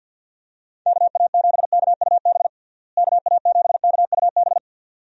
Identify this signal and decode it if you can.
Morse code, heard as a keyed tone, about 49 wpm, 700 Hz. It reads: KA7KUB KA7KUB
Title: USA Call Signs - 6 Characters - Repeated 2x - 50wpm